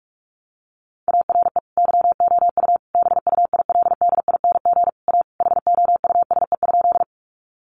Morse code, heard as a keyed tone, about 44 wpm, 700 Hz. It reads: ARE YOU BUILDING A HOUSE?
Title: Sentences from Top 700 Words 45wpm